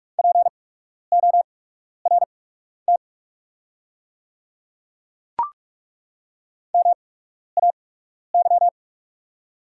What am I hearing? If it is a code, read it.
Morse code, heard as a keyed tone, about 42 wpm, spaced as for 12 wpm, 700 Hz. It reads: PORT MAY
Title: 8x Character Spacing - Top 300 Words - 45wpm